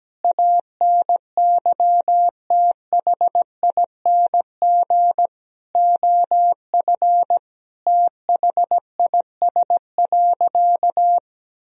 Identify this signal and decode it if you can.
Morse code, heard as a keyed tone, about 17 wpm, 700 Hz. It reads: ANYTHING OF THIS.
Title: Sentences from Top 1000 Words - Review Part 3 - 17wpm